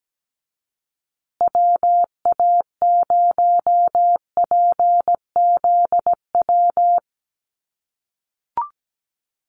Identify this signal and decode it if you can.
Morse code, heard as a keyed tone, about 17 wpm, 700 Hz. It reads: WA0PZW
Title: USA Call Signs Repeated 3x 17wpm